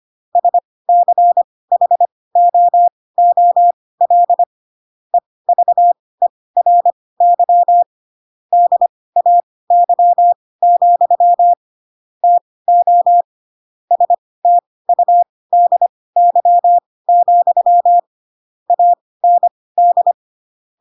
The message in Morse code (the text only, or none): SCHOOL EVERY DAY, TO STUDY, AND